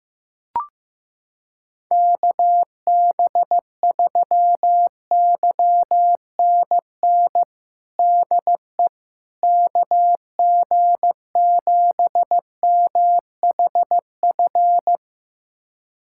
Morse code, heard as a keyed tone, about 15 wpm, 700 Hz. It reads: KB3YNN DE KG7MHF
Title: USA Callsign DE Callsign - 6 Characters - 15wpm